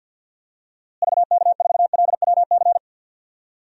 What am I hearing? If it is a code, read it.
Morse code, heard as a keyed tone, about 48 wpm, 700 Hz. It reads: VK4LWC
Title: DXCC-100 Call Signs - 6 Characters - Repeated 3x - 50wpm